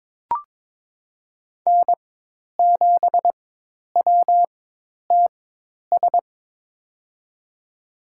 Morse code, heard as a keyed tone, about 22 wpm, spaced as for 10 wpm, 700 Hz. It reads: N7WTS
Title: USA Call Signs - 4x Character Spacing - 22wpm